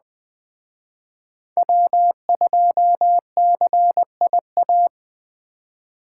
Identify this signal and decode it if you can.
Morse code, heard as a keyed tone, about 21 wpm, 700 Hz. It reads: W2CIA
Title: Speed Racing - USA 1x3 Call Signs 17wpm